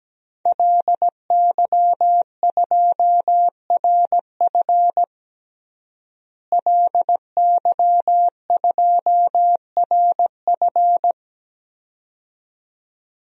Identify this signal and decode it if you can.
Morse code, heard as a keyed tone, about 17 wpm, 700 Hz. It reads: LY2RF LY2RF
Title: DXCC-100 Call Signs - 5 Characters - Repeated 2x - 17wpm